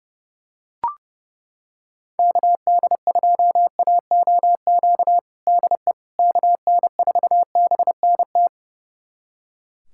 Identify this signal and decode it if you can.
Morse code, heard as a keyed tone, about 32 wpm, 700 Hz. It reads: KD2AOQ DE KN4BNT
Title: USA Callsign DE Callsign - 6 Characters - 30wpm